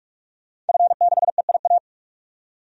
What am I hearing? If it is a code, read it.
Morse code, heard as a keyed tone, about 46 wpm, 700 Hz. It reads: F6EIA
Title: DXCC-100 Call Signs - 5 Characters - 45wpm